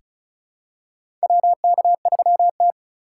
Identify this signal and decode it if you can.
Morse code, heard as a keyed tone, about 34 wpm, 700 Hz. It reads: WK3T
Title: USA Call Signs - 4 Characters - 3x Character Spacing - 35wpm